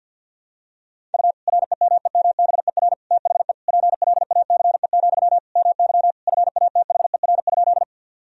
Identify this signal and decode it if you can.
Morse code, heard as a keyed tone, about 51 wpm, 700 Hz. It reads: U REMEMBER THE PLACE, MY FATHER?
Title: Sentences from Top 400 Words and Abbreviations 50wpm